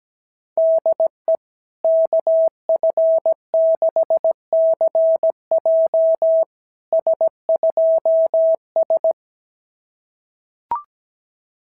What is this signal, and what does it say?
Morse code, heard as a keyed tone, about 17 wpm, 650 Hz. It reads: DE KF6CJ S2S
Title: SOTA QSO Practice 17wpm